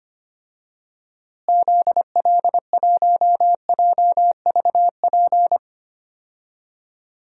Morse code, heard as a keyed tone, about 25 wpm, 700 Hz. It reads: ZL1JVP
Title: DXCC-100 Call Signs - 6 Characters - Repeated 3x - 25wpm